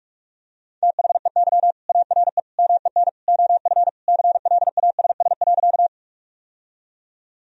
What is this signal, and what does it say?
Morse code, heard as a keyed tone, about 43 wpm, 700 Hz. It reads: THEY ARE MEN OF CLASS.
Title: Sentences from Top 1000 Words - Review Part 2 - 45wpm